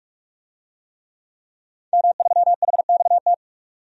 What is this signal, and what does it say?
Morse code, heard as a keyed tone, about 47 wpm, 700 Hz. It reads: M3HXT
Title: DXCC-100 Call Signs - 5 Characters - 45wpm